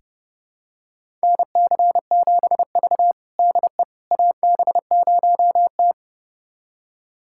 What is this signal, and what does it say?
Morse code, heard as a keyed tone, about 30 wpm, 700 Hz. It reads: NC7V DE AB0T
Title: USA Callsign DE Callsign - 4 Characters - 30wpm